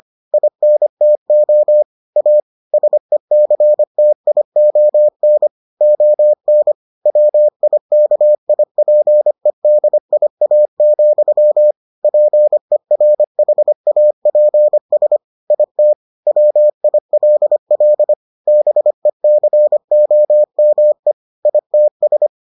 Morse code, heard as a keyed tone, about 25 wpm, 600 Hz. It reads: INTO A SECTION ON WIKIPEDIA, PERHAPS IT WILL BECOME ITS